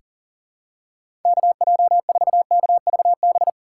Morse code, heard as a keyed tone, about 40 wpm, 700 Hz. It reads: KJ4KVB
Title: USA Call Signs - 3x Character Spacing - 40wpm